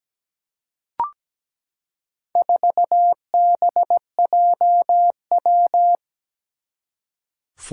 Morse code, heard as a keyed tone, about 17 wpm, 700 Hz. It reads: 4BJW
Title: Four Challenging Letter-Number Combinations 17wpm